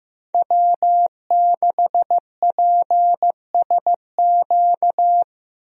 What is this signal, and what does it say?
Morse code, heard as a keyed tone, about 15 wpm, 700 Hz. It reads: W6PSQ
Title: USA Call Signs - 5 Characters - Repeated 3x - 15wpm